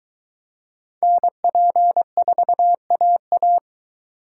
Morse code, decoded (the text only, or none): NP4AA